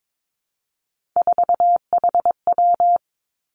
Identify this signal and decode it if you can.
Morse code, heard as a keyed tone, about 22 wpm, 700 Hz. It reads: 4HW
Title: Three Challenging Letter-Number Combinations 22wpm